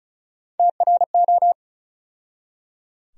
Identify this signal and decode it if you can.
Morse code, heard as a keyed tone, about 34 wpm, 700 Hz. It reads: TRO
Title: USA Call Sign Postfixes - 3 Letters 35wpm